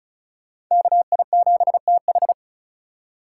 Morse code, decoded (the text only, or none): KI7TH